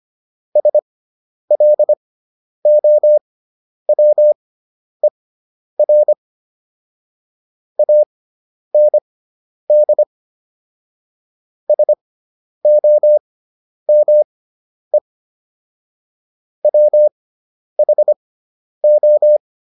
Morse code, heard as a keyed tone, about 25 wpm, spaced as for 10 wpm, 600 Hz. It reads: SLOWER AND SOME WHO